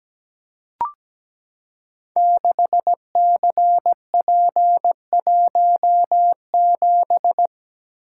Morse code, heard as a keyed tone, about 17 wpm, 700 Hz. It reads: 6CP17
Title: Code Groups - 5 Letter:Numbers - 17wpm